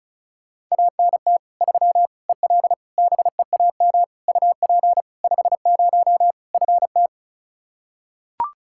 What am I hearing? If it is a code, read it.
Morse code, heard as a keyed tone, about 36 wpm, 700 Hz. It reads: ANT 3 EL BEAM UP 50 FT
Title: Practice QSO - Exchange 2 - Info - 35wpm